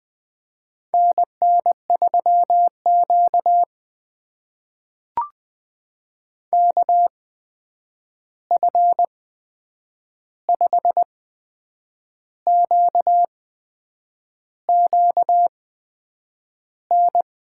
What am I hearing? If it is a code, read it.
Morse code, heard as a keyed tone, about 20 wpm, spaced as for 8 wpm, 700 Hz. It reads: NN3Q KF5QQN
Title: USA Call Signs - 8x Character Spacing - 20wpm